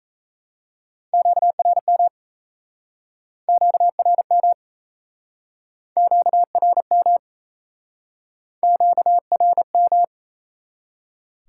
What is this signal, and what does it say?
Morse code, heard as a keyed tone, about 37 wpm, 700 Hz. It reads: QRM QRM QRM QRM